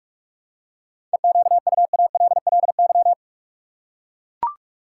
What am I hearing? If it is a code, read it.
Morse code, heard as a keyed tone, about 43 wpm, 700 Hz. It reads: EQUALLY
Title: Seven Letter Words - 45wpm